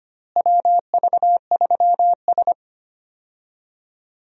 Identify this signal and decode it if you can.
Morse code, heard as a keyed tone, about 25 wpm, 700 Hz. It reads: WV3S
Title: USA Call Signs - 4 Characters - Repeated 3x - 25wpm